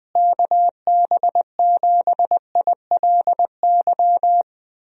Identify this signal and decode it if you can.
Morse code, heard as a keyed tone, about 20 wpm, 700 Hz. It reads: KB7ILY
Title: USA Call Signs - 6 Characters - Repeated 3x - 20wpm